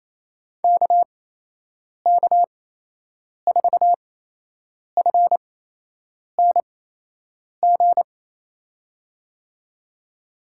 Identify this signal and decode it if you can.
Morse code, heard as a keyed tone, about 28 wpm, spaced as for 8 wpm, 700 Hz. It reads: KK4FNG
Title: USA Call Signs - 6 Characters - 8x Character Spacing - 28wpm